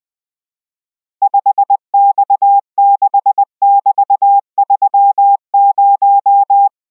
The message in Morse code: 5X6<BT>30